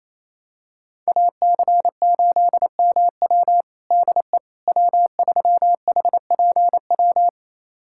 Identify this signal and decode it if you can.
Morse code, heard as a keyed tone, about 28 wpm, 700 Hz. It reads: AC8MW DE W3HPW
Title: USA Callsign DE Callsign - 5 Characters - 28wpm